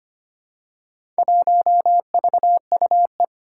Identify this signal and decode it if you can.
Morse code, heard as a keyed tone, about 25 wpm, 700 Hz. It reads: 1VUE
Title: Four Challenging Letter-Number Combinations 25wpm